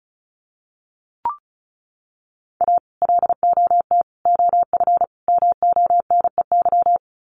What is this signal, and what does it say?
Morse code, heard as a keyed tone, about 34 wpm, 700 Hz. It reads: A LOT OF MONEY
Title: Sets of 4 Words 35wpm